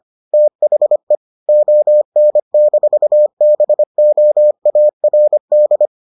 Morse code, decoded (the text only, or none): THE ON-BOARD